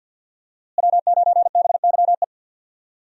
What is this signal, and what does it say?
Morse code, heard as a keyed tone, about 47 wpm, 700 Hz. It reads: W9BYE
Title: USA Call Signs - 5 Characters - Repeated 3x - 50wpm